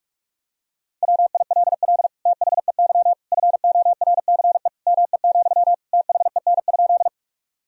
Morse code, heard as a keyed tone, about 46 wpm, 700 Hz. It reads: WILL THEY FORCE ME, THEN?
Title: Sentences from Top 1000 Words - Review Part 3 - 45wpm